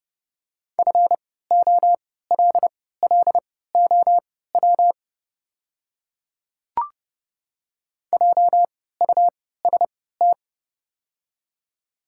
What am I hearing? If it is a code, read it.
Morse code, heard as a keyed tone, about 30 wpm, spaced as for 17 wpm, 700 Hz. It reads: FOLLOW JUST
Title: 3x Character Spacing - Top 200 Words - 30wpm